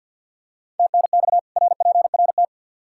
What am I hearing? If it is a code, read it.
Morse code, heard as a keyed tone, about 47 wpm, 700 Hz. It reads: TNX RPRT
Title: Practice QSO - Exchange 2 - Info - 50wpm